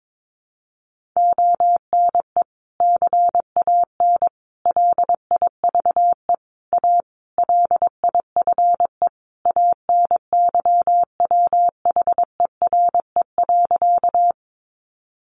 Morse code, heard as a keyed tone, about 22 wpm, 700 Hz. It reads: ONE CAN LIVE A LIFE ANYWHERE.
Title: Sentences from Top 1400 Words 22wpm